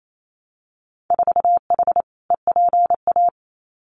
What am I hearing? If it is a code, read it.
Morse code, heard as a keyed tone, about 28 wpm, 700 Hz. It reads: 4H EPA
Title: Winter Field Day Exchanges 28wpm